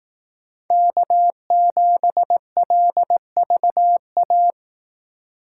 Morse code, heard as a keyed tone, about 18 wpm, 700 Hz. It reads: K7LVA
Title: Speed Racing - USA 1x3 Call Signs 15wpm